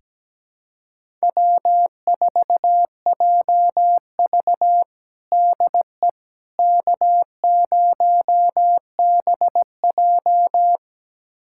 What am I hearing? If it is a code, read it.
Morse code, heard as a keyed tone, about 17 wpm, 700 Hz. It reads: W4JV DE K0BJ
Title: USA Callsign DE Callsign - 4 Characters - 17wpm